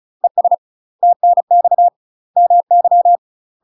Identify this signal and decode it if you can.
Morse code, heard as a keyed tone, about 36 wpm, 700 Hz. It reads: ES TNX MY
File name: UK MASTER QSOs TRAINING / 35WPM / UK MASTER QSOs 0015